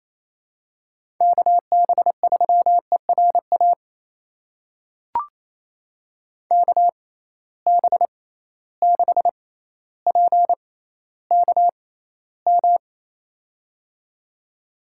Morse code, heard as a keyed tone, about 27 wpm, spaced as for 14 wpm, 700 Hz. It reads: KB3ERA KB6PKM
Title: USA Call Signs - 6 Characters - 6x Character Spacing - 28wpm